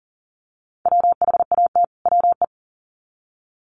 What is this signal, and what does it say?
Morse code, heard as a keyed tone, about 40 wpm, 700 Hz. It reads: WHAT WE
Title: Sets of 2 Words 40wpm